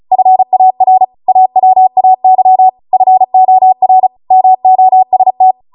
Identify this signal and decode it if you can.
Morse code, heard as a keyed tone, about 36 wpm, 750 Hz. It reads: FAR AWAY FOR MOST